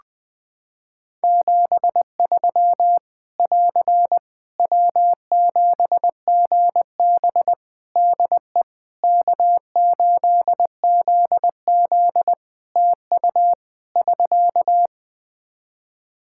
Morse code, heard as a keyed tone, about 21 wpm, 700 Hz. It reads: 73 <AR> W7GB DE K8ZZ TU <SK>
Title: Practice QSO - Exchange 4 - Ending - 20wpm